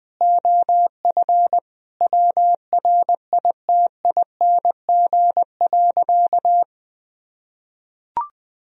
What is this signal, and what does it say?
Morse code, heard as a keyed tone, about 21 wpm, 700 Hz. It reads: OF WRITING.
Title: Sentences from Top 1000 Words - Review Part 2 - 20wpm